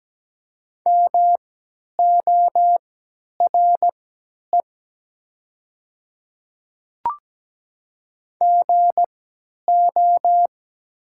Morse code, heard as a keyed tone, about 17 wpm, spaced as for 10 wpm, 700 Hz. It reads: MORE GO
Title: 3x Character Spacing - Top 100 Words - 17wpm